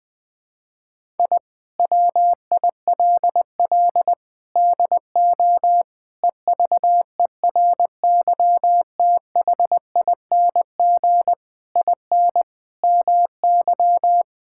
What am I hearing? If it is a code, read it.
Morse code, heard as a keyed tone, about 20 wpm, 700 Hz. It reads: I WILL DO EVERYTHING IN MY
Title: Sentences with 8-Word Phrases - 20wpm